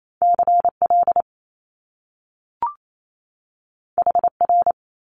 Morse code, heard as a keyed tone, about 28 wpm, 700 Hz. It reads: CL HR
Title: Top Two Letter Combinations in Words 28wpm